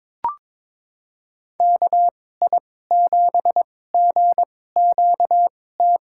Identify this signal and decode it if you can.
Morse code, heard as a keyed tone, about 22 wpm, spaced as for 16 wpm, 700 Hz. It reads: KI7GQT
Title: USA Call Signs - 6 Characters - 2x Character Spacing - 22wpm